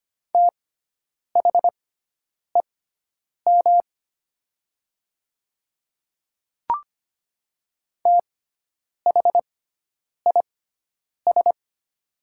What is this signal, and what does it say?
Morse code, heard as a keyed tone, about 25 wpm, spaced as for 9 wpm, 700 Hz. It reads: THEM THIS
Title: 6x Character Spacing - Top 100 Words - 25wpm